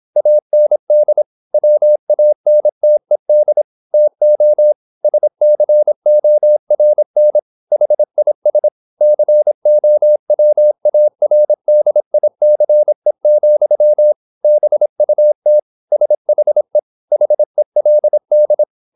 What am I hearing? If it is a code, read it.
Morse code, heard as a keyed tone, about 26 wpm, 600 Hz. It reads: AND WANTED TO SCORN HIS COWARDICE, BUT SHE HELD
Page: Never 26wpm | CW med Gnister